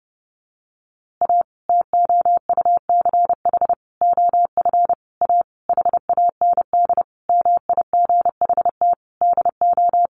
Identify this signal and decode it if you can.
Morse code, heard as a keyed tone, about 30 wpm, 700 Hz. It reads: A TOUCH OF A HAND MIGHT DO
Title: A Sound Of Thunder 30wpm